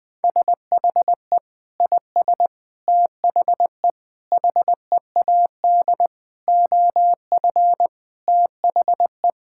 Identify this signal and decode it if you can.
Morse code, heard as a keyed tone, about 20 wpm, 700 Hz. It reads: SHE IS THE HEAD OF THE
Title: Sentences with 4-Word Phrases - Spoken Beginning - 20wpm